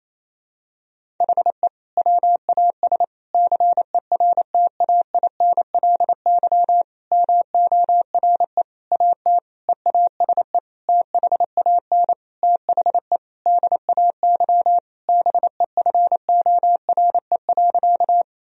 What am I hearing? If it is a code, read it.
Morse code, heard as a keyed tone, about 27 wpm, 700 Hz. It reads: HE WAS CERTAINLY MORE AT EASE THAN THE DAY BEFORE.
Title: Sentences from Top 1000 Words - Review Part 1 - 28wpm